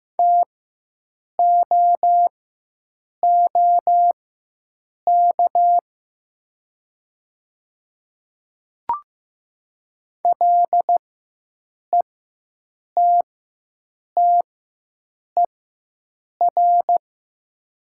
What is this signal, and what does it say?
Morse code, heard as a keyed tone, about 15 wpm, spaced as for 7 wpm, 690 Hz. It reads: TOOK LETTER
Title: 4x Character Spacing - Top 500 Words - 15wpm